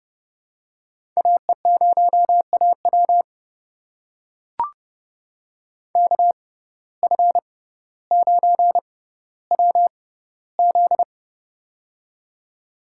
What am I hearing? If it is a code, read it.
Morse code, heard as a keyed tone, about 32 wpm, spaced as for 16 wpm, 700 Hz. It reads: AE0AW KF9WZ
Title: USA Call Signs - 5 Characters - 6x Character Spacing - 30wpm